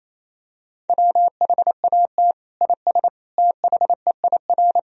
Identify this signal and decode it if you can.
Morse code, heard as a keyed tone, about 28 wpm, 700 Hz. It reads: WHAT IS THEIR
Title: Sentences from Top 1000 Words - Review Part 1 - 28wpm